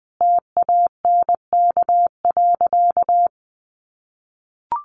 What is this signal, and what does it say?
Morse code, heard as a keyed tone, about 20 wpm, 700 Hz. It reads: TANK.
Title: Additional Easy Phrases 20wpm